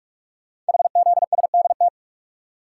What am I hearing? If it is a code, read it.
Morse code, heard as a keyed tone, about 44 wpm, 700 Hz. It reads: HZSDT
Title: Code Groups - 5 Letters - 45wpm